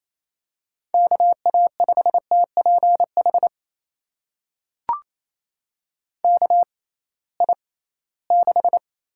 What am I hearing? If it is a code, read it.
Morse code, heard as a keyed tone, about 28 wpm, spaced as for 18 wpm, 700 Hz. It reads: KA5TPH KI6
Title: USA Call Signs - 6 Characters - 6x Character Spacing - 28wpm